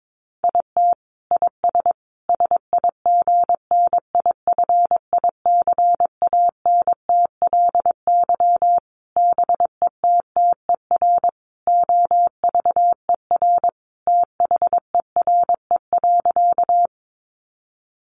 Morse code, heard as a keyed tone, about 22 wpm, 700 Hz. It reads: IT IS SIGNIFICANTLY BETTER OVER THERE.
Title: Sentences from Top 1500 Words 22wpm